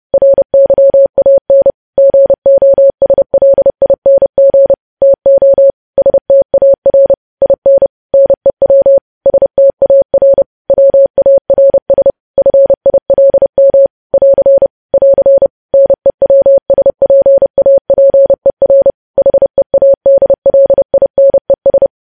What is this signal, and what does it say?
Morse code, heard as a keyed tone, about 30 wpm, 550 Hz. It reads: RYAN GOSLING TO STAR IN NEW STAR WARS FILM <AR> <AR> NEWSPAPER HEADLINES
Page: News Headlines in Morse Code at 30 WPM